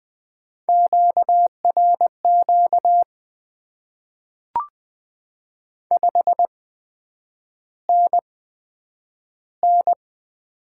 Morse code, heard as a keyed tone, about 20 wpm, spaced as for 9 wpm, 700 Hz. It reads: QRQ 5NN
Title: Top 90 QSO Elements - 8x Character Spacing - 20wpm